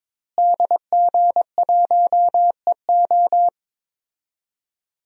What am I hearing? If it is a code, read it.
Morse code, heard as a keyed tone, about 22 wpm, 700 Hz. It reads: DG1EO
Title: DXCC-100 Call Signs - 5 Characters - Repeated 3x - 22wpm